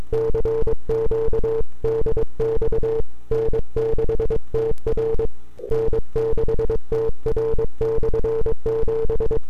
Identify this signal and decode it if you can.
Morse code, heard as a keyed tone, about 22 wpm, 440 Hz. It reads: CQ DX N6TR N6TR/7
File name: n6tr / 12Oct2009-160